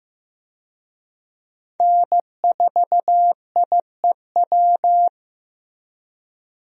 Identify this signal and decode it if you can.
Morse code, heard as a keyed tone, about 15 wpm, 700 Hz. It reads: N4IEW
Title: USA Call Signs - 5 Characters - Repeated 2x - 15wpm